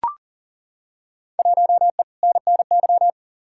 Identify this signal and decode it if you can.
Morse code, heard as a keyed tone, about 40 wpm, 700 Hz. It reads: 1E NNY
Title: Field Day Exchanges - 40wpm